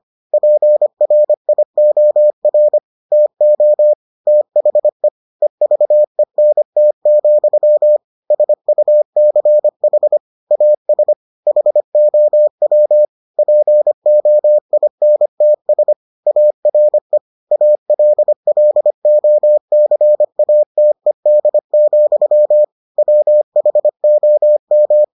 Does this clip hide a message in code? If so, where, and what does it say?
Yes: Morse code from 0.3 to 25.0 s, heard as a keyed tone, about 25 wpm, 600 Hz. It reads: PRIOR TO THE EVENT, SUCH AS HOW POINTS ARE ALLOCATED, WHOM